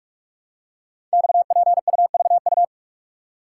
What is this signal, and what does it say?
Morse code, heard as a keyed tone, about 44 wpm, 700 Hz. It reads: XPUVU
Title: Code Groups - 5 Letters - 45wpm